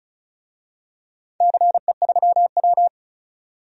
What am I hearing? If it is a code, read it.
Morse code, heard as a keyed tone, about 36 wpm, 700 Hz. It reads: CE3W
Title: DXCC-100 Call Signs - 4 Characters - 35wpm